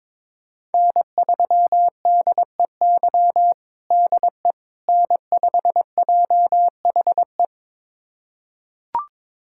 Morse code, heard as a keyed tone, about 22 wpm, 700 Hz. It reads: N3DEY DE N5JHE
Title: USA Callsign DE Callsign - 5 Characters - 22wpm